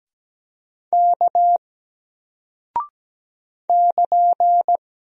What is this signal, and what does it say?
Morse code, heard as a keyed tone, about 17 wpm, 700 Hz. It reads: K <KN>
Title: Single Character (Letter:Number:Punctuation:Prosign) Alt 17wpm